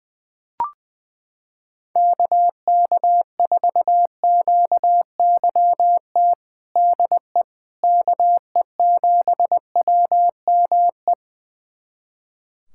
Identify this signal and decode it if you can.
Morse code, heard as a keyed tone, about 20 wpm, 700 Hz. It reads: KK4QYT DE KE7WME
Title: USA Callsign DE Callsign - 6 Characters - 20wpm